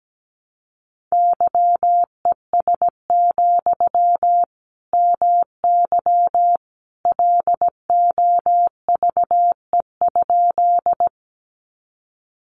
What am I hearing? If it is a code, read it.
Morse code, heard as a keyed tone, about 17 wpm, 700 Hz. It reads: YES, MY LOVE?
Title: Sentences from Top 500 Words 17wpm